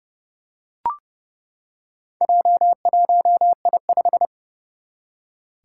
Morse code, heard as a keyed tone, about 29 wpm, 700 Hz. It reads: J1I5
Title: Four Challenging Letter-Number Combinations 30wpm